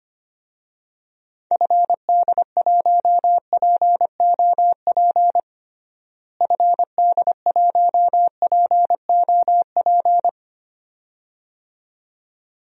Morse code, heard as a keyed tone, about 25 wpm, 700 Hz. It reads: FD1POP FD1POP
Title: DXCC-100 Call Signs - 6 Characters - Repeated 2x - 25wpm